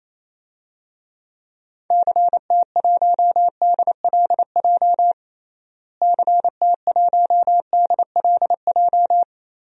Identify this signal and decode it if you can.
Morse code, heard as a keyed tone, about 28 wpm, 700 Hz. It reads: CT1DLJ CT1DLJ
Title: DXCC-100 Call Signs - 6 Characters - Repeated 3x - 28wpm